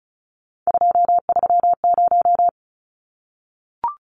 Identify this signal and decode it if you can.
Morse code, heard as a keyed tone, about 34 wpm, 700 Hz. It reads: 230
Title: Numbers - 3-Digit 35wpm